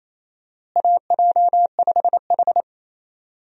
Morse code, heard as a keyed tone, about 28 wpm, 700 Hz. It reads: AJ5H
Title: USA Call Signs - 4 Characters - Repeated 2x - 28wpm